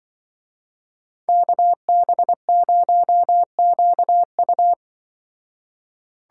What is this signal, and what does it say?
Morse code, heard as a keyed tone, about 23 wpm, 700 Hz. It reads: KB0QU
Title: Speed Racing - USA Call Signs 20wpm